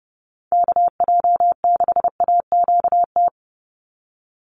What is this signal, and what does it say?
Morse code, heard as a keyed tone, about 30 wpm, 700 Hz. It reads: KJ6AQT
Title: USA Call Signs 30wpm